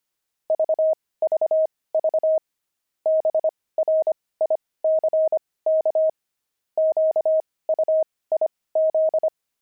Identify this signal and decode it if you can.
Morse code, heard as a keyed tone, about 25 wpm, spaced as for 18 wpm, 630 Hz. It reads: VVV BRICK QUIZ
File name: pangram-fs-18-cs-25-veldt